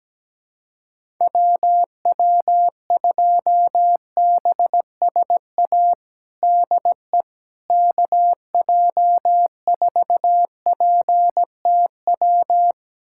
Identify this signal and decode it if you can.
Morse code, heard as a keyed tone, about 17 wpm, 700 Hz. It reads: WW2BSA DE KJ4PTW
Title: USA Callsign DE Callsign - 6 Characters - 17wpm